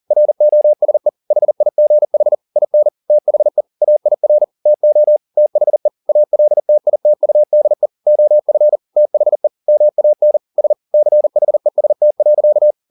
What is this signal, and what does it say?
Morse code, heard as a keyed tone, about 40 wpm, 600 Hz. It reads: ROSE HIGH IN THE AIR TO THE ALTITUDE OF THE MAN S CHEST.
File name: the old man and the sea 2 40wpm